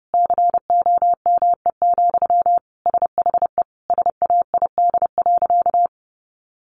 Morse code, heard as a keyed tone, about 30 wpm, 700 Hz. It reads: COME, SHE SAID.
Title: Sentences from Top 100 Words 30wpm